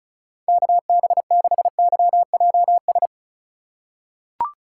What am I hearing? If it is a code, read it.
Morse code, heard as a keyed tone, about 36 wpm, 700 Hz. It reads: KB6YJS
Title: USA Call Signs - 4x Character Spacing - 35wpm